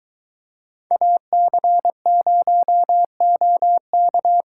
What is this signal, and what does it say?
Morse code, heard as a keyed tone, about 23 wpm, 700 Hz. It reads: AC0OK